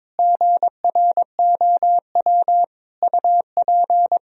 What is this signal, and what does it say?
Morse code, heard as a keyed tone, about 21 wpm, 700 Hz. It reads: GROW UP
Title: Sentences from Top 1000 Words - Review Part 1 - 22wpm